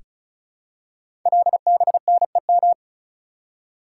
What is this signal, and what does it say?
Morse code, heard as a keyed tone, about 36 wpm, 690 Hz. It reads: LBNEM
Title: Code Groups - 5 Letters - 35wpm